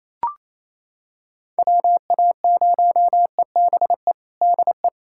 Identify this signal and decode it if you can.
Morse code, heard as a keyed tone, about 28 wpm, 700 Hz. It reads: WA0EBE DE
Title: USA Callsign DE Callsign - 6 Characters - 28wpm